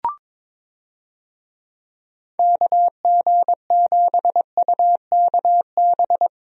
Morse code, heard as a keyed tone, about 22 wpm, 700 Hz. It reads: KG7UKB
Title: USA Call Signs - 6 Characters - Repeated 3x - 22wpm